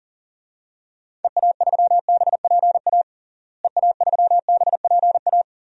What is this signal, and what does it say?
Morse code, heard as a keyed tone, about 40 wpm, 700 Hz. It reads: EA3BPA EA3BPA
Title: DXCC-100 Call Signs - 6 Characters - Repeated 2x - 40wpm